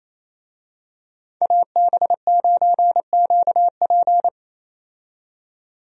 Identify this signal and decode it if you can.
Morse code, heard as a keyed tone, about 28 wpm, 700 Hz. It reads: AB9QP
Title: USA Call Signs - 5 Characters - 28wpm